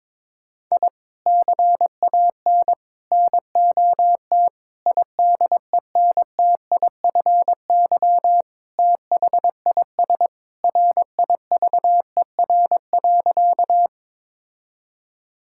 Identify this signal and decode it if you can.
Morse code, heard as a keyed tone, about 22 wpm, 700 Hz. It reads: I CAN NOT IDENTIFY THIS RIVER.
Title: Sentences from Top 1000 Words - Review Part 6 - 22wpm